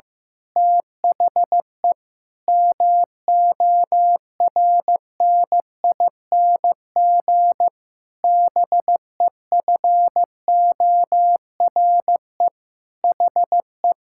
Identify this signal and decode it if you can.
Morse code, heard as a keyed tone, about 16 wpm, 700 Hz. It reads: THE MORNING BEFORE HE
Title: Pangram Challenge v2 - 15wpm